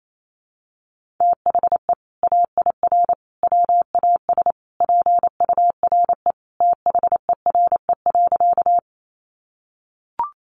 Morse code, heard as a keyed tone, about 28 wpm, 700 Hz. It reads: THE AIR WAS PURE THERE.
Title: Sentences from Top 1500 Words 28wpm